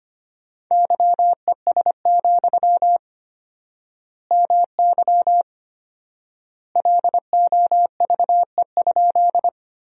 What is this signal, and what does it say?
Morse code, heard as a keyed tone, about 25 wpm, 700 Hz. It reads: YES, MY LOVE?
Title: 4x Word-Spacing - Sentences from Top 500 Words - 25wpm